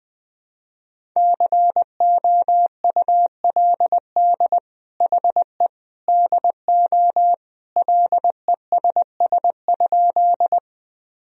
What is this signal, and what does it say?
Morse code, heard as a keyed tone, about 21 wpm, 700 Hz. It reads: COULD HE DO LESS?
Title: Sentences from Top 1000 Words - Review Part 2 - 20wpm